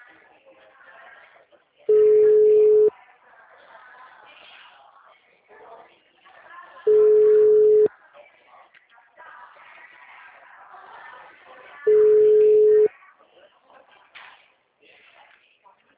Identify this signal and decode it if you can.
Ringback tone (Europe, 425 Hz), with speech.
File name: out- -2019- - - .69861